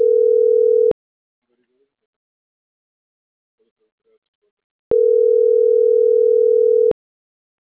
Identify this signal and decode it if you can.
Ringback tone (North America), 440 + 480 Hz.